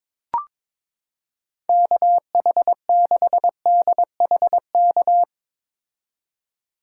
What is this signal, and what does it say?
Morse code, heard as a keyed tone, about 22 wpm, 700 Hz. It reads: KH6DHK
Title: K1USN SST QSO Practice - 22wpm